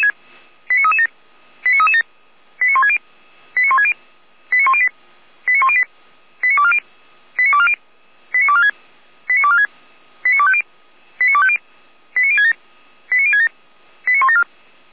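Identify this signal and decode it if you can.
Five-tone selective call (ZVEI), heard as quick runs of five tones, tones of about 70 ms. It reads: 89337 89337 89160 89160 89108 89108 89240 89240 89246 89246 89250 89250 89067 89067 89174
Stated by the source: ZVEI five-tone selective call received with an SDR.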